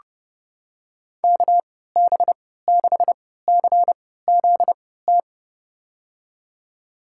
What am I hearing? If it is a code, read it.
Morse code, heard as a keyed tone, about 32 wpm, spaced as for 17 wpm, 700 Hz. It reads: KB6CZT
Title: USA Call Signs - 6 Characters - 3x Character Spacing - 30wpm